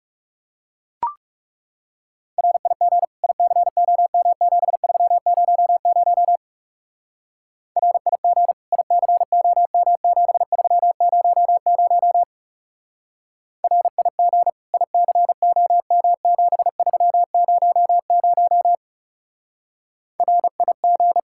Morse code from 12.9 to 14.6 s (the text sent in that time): RIG